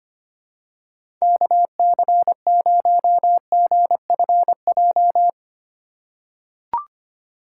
Morse code, heard as a keyed tone, about 25 wpm, 700 Hz. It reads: KC0GFJ
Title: USA Call Signs - 6 Characters - 6x Character Spacing - 25wpm